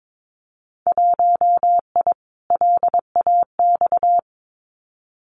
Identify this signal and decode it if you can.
Morse code, heard as a keyed tone, about 22 wpm, 700 Hz. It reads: 1I LAX
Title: Winter Field Day Exchanges Encore 22wpm